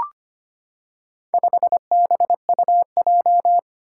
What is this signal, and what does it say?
Morse code, heard as a keyed tone, about 25 wpm, 700 Hz. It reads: 5BUJ
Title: Four Challenging Letter-Number Combinations 25wpm